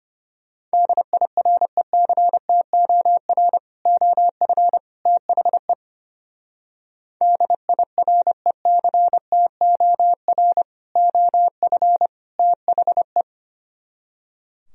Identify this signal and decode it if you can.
Morse code, heard as a keyed tone, about 29 wpm, 700 Hz. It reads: DIRECTOR OF THE DIRECTOR OF THE
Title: Speed-Racing - Sets of 3 Words 25wpm